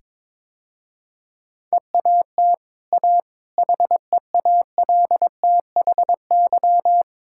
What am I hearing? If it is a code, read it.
Morse code, heard as a keyed tone, about 22 wpm, 690 Hz. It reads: EAT A HEALTHY
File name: practice / 22wpm / Easy Sentences with Hints - 22wpm